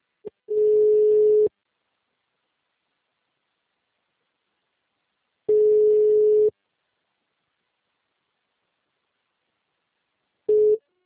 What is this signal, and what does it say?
Ringback tone (Europe, 425 Hz).